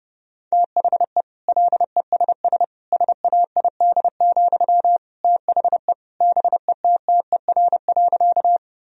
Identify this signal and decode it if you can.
Morse code, heard as a keyed tone, about 32 wpm, 700 Hz. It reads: THE LESS SAID, THE BETTER.
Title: Sentences from Top 1000 Words - Review Part 2 - 30wpm